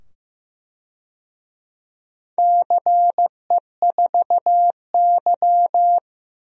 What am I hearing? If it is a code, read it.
Morse code, heard as a keyed tone, about 15 wpm, 700 Hz. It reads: CE4Y
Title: DXCC-100 Call Signs - 4 Characters - 15wpm